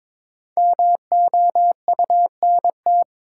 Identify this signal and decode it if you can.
Morse code, heard as a keyed tone, about 23 wpm, 700 Hz. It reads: MOUNT
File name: 5 Letter Words - First 3 Letters Given - 22wpm